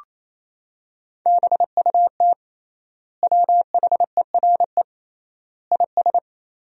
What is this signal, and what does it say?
Morse code, heard as a keyed tone, about 28 wpm, 700 Hz. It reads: BUT WHERE IS
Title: 3x Word-Spacing - Sentences from Top 200 Words - 28wpm